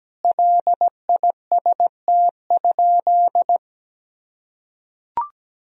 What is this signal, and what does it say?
Morse code, heard as a keyed tone, about 17 wpm, 700 Hz. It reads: LIST?
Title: Sentences from Top 1000 Words - Review Part 2 - 17wpm